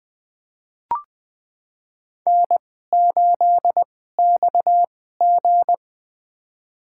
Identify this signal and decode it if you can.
Morse code, heard as a keyed tone, about 20 wpm, spaced as for 14 wpm, 700 Hz. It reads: N8XG
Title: USA Call Signs - 4 Characters - 2x Character Spacing - 20wpm